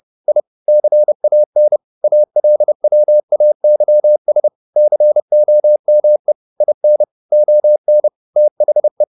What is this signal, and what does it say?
Morse code, heard as a keyed tone, about 30 wpm, 600 Hz. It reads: I CAN ALWAYS COME IN ON THE
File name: the old man and the sea / the old man and the sea 1 30wpm